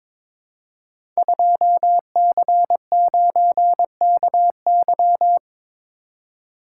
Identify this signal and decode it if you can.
Morse code, heard as a keyed tone, about 22 wpm, 700 Hz. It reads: 2C9KY
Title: Code Groups - 5 Letter:Numbers - 22wpm